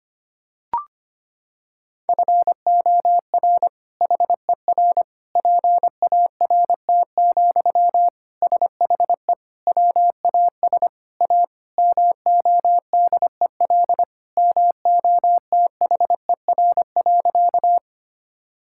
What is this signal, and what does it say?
Morse code, heard as a keyed tone, about 25 wpm, 700 Hz. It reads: FOR HER PART, SHE WAS A MODEL MOTHER.
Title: Sentences from Top 1000 Words - Review Part 5 - 25wpm